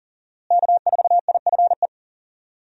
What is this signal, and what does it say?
Morse code, heard as a keyed tone, about 40 wpm, 700 Hz. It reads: K4IFE
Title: USA Call Signs - 5 Characters - Repeated 2x - 40wpm